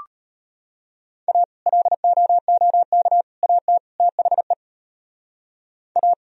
Morse code, heard as a keyed tone, about 33 wpm, 700 Hz. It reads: A LOOK AT THE A